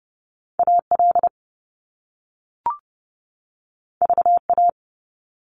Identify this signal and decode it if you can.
Morse code, heard as a keyed tone, about 30 wpm, 700 Hz. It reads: AL VA
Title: US States 30wpm